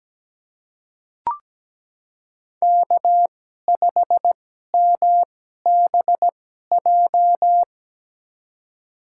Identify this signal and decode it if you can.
Morse code, heard as a keyed tone, about 17 wpm, spaced as for 12 wpm, 700 Hz. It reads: K5MBJ
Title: USA Call Signs - 5 Characters - 2x Character Spacing - 17wpm